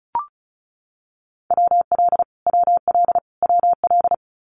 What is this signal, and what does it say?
Morse code, heard as a keyed tone, about 34 wpm, 700 Hz. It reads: WL WL WL
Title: Abbreviations Repeated 3x - 35wpm